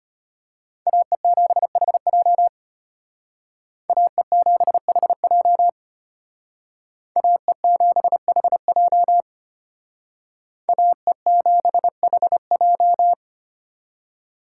Morse code, heard as a keyed tone, about 32 wpm, 700 Hz. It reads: AE7HJ AE7HJ AE7HJ AE7HJ